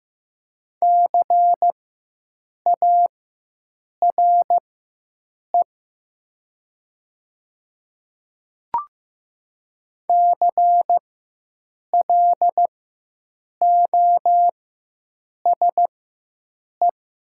Morse code, heard as a keyed tone, about 15 wpm, spaced as for 7 wpm, 700 Hz. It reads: CARE CLOSE
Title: 4x Character Spacing - Top 300 Words - 15wpm